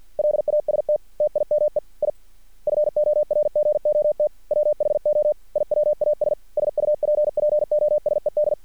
Morse code, heard as a keyed tone, about 50 wpm, 600 Hz. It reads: LAST TIME I FORGOT WHO I WAS SUPPOSED
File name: archived / CRSnet